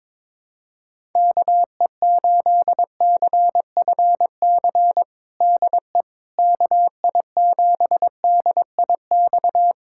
Morse code, heard as a keyed tone, about 22 wpm, 700 Hz. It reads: KE8CFC DE KI7DIX
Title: USA Callsign DE Callsign - 6 Characters - 22wpm